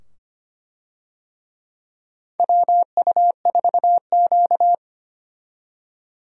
Morse code, heard as a keyed tone, about 25 wpm, 700 Hz. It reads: WU4Q
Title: USA Call Signs - 4 Characters - 25wpm